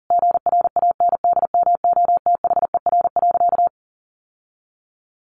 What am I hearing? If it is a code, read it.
Morse code, heard as a keyed tone, about 40 wpm, 700 Hz. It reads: GRANDMOTHER.
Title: Sentences from Top 1200 Words 40wpm